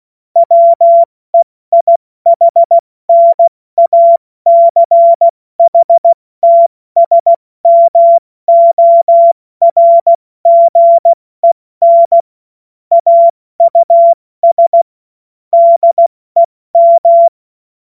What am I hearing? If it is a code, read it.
Morse code, heard as a keyed tone, about 16 wpm, 680 Hz. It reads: WEIHNACHTSMORGEN AUS DEM
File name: Morsekurs/Witze